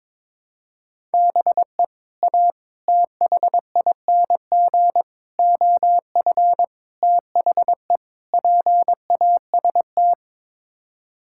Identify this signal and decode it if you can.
Morse code, heard as a keyed tone, about 21 wpm, 700 Hz. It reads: BE A THING OF THE PAST
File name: practice / 22wpm / Sets of 6 Words - 22wpm